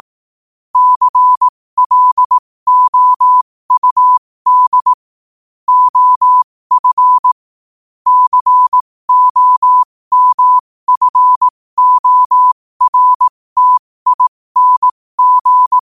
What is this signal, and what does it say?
Morse code, heard as a keyed tone, about 18 wpm, spaced as for 15 wpm, 1000 Hz. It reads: CLOUD OF COMFORTING
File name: qotdfiles / 15wpm